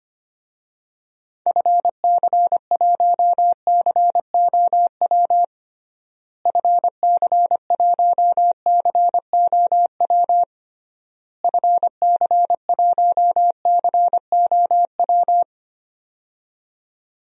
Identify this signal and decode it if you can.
Morse code, heard as a keyed tone, about 25 wpm, 700 Hz. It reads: FC1COW FC1COW FC1COW
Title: DXCC-100 Call Signs - 6 Characters - Repeated 3x - 25wpm